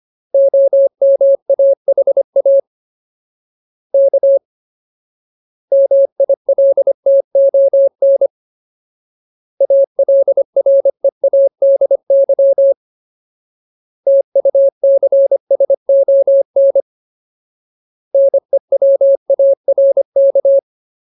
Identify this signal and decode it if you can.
Morse code, heard as a keyed tone, about 25 wpm, 550 Hz. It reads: OMAHA K MILTON ALREADY TUCSON NEWARK
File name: Ultimate Word List / 25WPM Ultimate